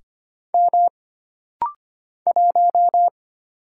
Morse code, heard as a keyed tone, about 25 wpm, 700 Hz. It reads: M 1
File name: Single Character (Letter:Number:Punctuation:Prosign) - Rapid-Fire - 25wpm